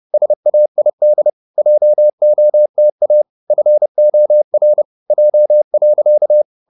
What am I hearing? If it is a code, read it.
Morse code, heard as a keyed tone, about 30 wpm, 600 Hz. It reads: SAID JOTA FOR J.
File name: the old man and the sea / the old man and the sea 1 30wpm